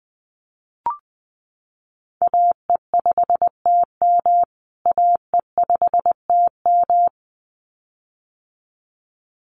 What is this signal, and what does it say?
Morse code, heard as a keyed tone, about 20 wpm, 700 Hz. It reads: AE5TM AE5TM
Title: USA Call Signs Repeated 2x 20wpm